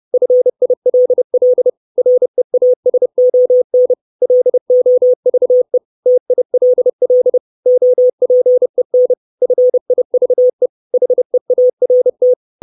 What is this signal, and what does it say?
Morse code, heard as a keyed tone, about 30 wpm, 500 Hz. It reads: FILL REASON LOVE TILL OPEN FIVE HEART
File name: 632 Words 30WPM